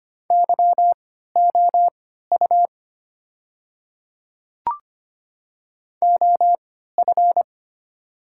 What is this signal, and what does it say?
Morse code, heard as a keyed tone, about 25 wpm, spaced as for 14 wpm, 700 Hz. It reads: YOU OF
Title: 3x Character Spacing - Top 100 Words - 25wpm